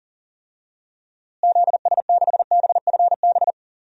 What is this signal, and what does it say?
Morse code, heard as a keyed tone, about 40 wpm, 700 Hz. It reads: ZS6BFB
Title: DXCC-100 Call Signs - 6 Characters - Repeated 2x - 40wpm